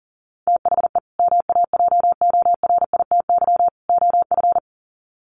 Morse code, heard as a keyed tone, about 40 wpm, 700 Hz. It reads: THE MAJORITY OF
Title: Sets of 3 Words 40wpm